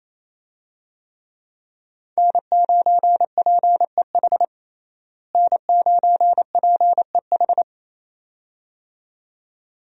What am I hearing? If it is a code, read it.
Morse code, heard as a keyed tone, about 28 wpm, 700 Hz. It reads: N9PEH N9PEH
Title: USA Call Signs - 5 Characters - Repeated 2x - 28wpm